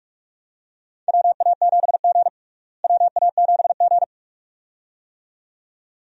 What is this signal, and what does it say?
Morse code, heard as a keyed tone, about 45 wpm, 700 Hz. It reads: WA7G WA7G
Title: USA Call Signs - 4 Characters - Repeated 2x - 45wpm